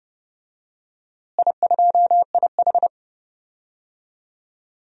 Morse code, heard as a keyed tone, about 30 wpm, 700 Hz. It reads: I2IH
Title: DXCC-100 Call Signs - 4 Characters - Repeated 2x - 30wpm